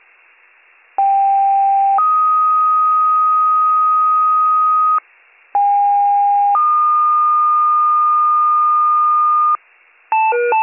Two-tone page, heard as a one-second tone, then a three-second tone, about 790 then 1250 Hz, about 810 then 1220 Hz. It is heard from 1.0 to 9.6 s.